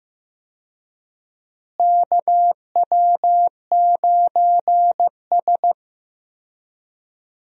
Morse code, heard as a keyed tone, about 15 wpm, 700 Hz. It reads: KW9S
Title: USA Call Signs - 4 Characters - Repeated 2x - 15wpm